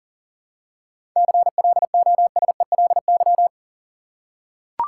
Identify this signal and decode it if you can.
Morse code, heard as a keyed tone, about 40 wpm, 700 Hz. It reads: CLOSELY
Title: Seven Letter Words - 40wpm